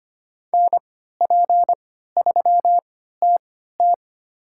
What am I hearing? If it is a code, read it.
Morse code, heard as a keyed tone, about 25 wpm, spaced as for 14 wpm, 700 Hz. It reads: NP3TT
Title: USA Call Signs - 5 Characters - 3x Character Spacing - 25wpm